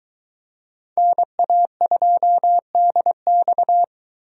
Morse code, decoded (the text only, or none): NA2DX